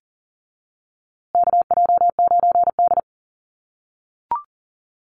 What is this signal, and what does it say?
Morse code, heard as a keyed tone, about 40 wpm, 700 Hz. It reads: KJ9D
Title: USA Call Signs Repeated 3x 40wpm